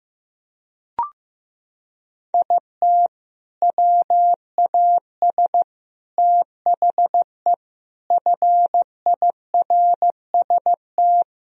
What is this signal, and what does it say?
Morse code, heard as a keyed tone, about 15 wpm, 700 Hz. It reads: IT WAS THE FIRST
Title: Sentences from Top 1000 Words - Review Part 2 - 15wpm